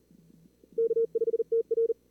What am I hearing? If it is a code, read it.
Morse code, heard as a keyed tone, about 40 wpm, 440 Hz. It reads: K5TR
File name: CWSprint / Feb2022 / k5tr-feb-na-sprint-cw-k5ot -2300